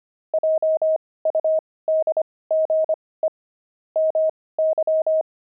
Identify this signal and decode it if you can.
Morse code, heard as a keyed tone, about 25 wpm, spaced as for 18 wpm, 630 Hz. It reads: JUDGE MY
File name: pangram-fs-18-cs-25-sphinx